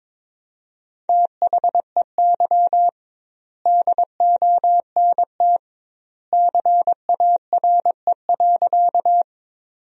Morse code, heard as a keyed tone, about 22 wpm, 700 Hz. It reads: THEY DONT CARE.
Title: 2x Word-Spacing - Sentences from Top 300 Words - 22wpm